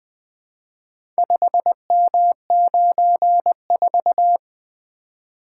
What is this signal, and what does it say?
Morse code, heard as a keyed tone, about 20 wpm, 700 Hz. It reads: 5M94
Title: Four Letter-Number Combinations - 20wpm